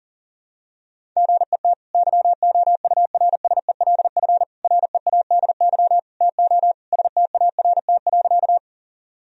Morse code, heard as a keyed tone, about 40 wpm, 700 Hz. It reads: GET YOURSELF READY TO START.
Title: Sentences from Top 1000 Words - Review Part 1 - 40wpm